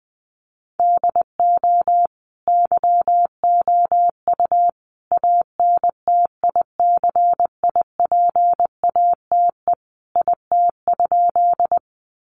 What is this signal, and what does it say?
Morse code, heard as a keyed tone, about 20 wpm, 700 Hz. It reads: DO YOU ANTICIPATE IT?
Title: Sentences from Top 1500 Words 20wpm